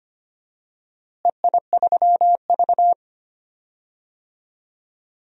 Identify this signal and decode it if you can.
Morse code, heard as a keyed tone, about 25 wpm, 700 Hz. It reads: EI3V
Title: DXCC-100 Call Signs - 4 Characters - Repeated 2x - 25wpm